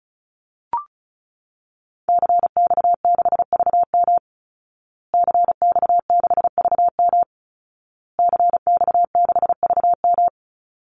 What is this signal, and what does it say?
Morse code, heard as a keyed tone, about 34 wpm, 700 Hz. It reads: CX6VM CX6VM CX6VM
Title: Call Signs Repeated 2x - 35wpm